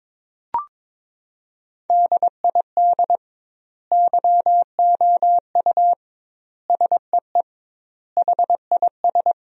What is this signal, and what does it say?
Morse code, heard as a keyed tone, about 22 wpm, 700 Hz. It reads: DID YOU SEE HIS
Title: 2x Word-Spacing - Sentences from Top 300 Words - 22wpm